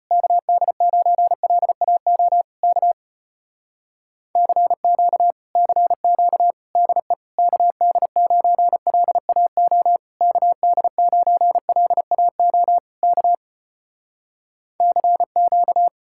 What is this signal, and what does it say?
Morse code, heard as a keyed tone, about 33 wpm, 700 Hz. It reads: KD9LAO K CQ CQ DE KD9LAO KD9LAO K CQ